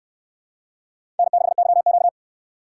Morse code, heard as a keyed tone, about 68 wpm, 700 Hz. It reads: N6YQ